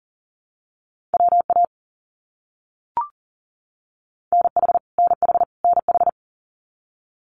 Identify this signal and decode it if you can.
Morse code, heard as a keyed tone, about 40 wpm, 700 Hz. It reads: WA NH NH NH
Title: US States Repeated 3x 40wpm